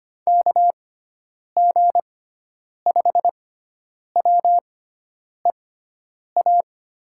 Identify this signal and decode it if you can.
Morse code, heard as a keyed tone, about 25 wpm, spaced as for 9 wpm, 700 Hz. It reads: KG5WEA
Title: USA Call Signs - 6 Characters - 6x Character Spacing - 25wpm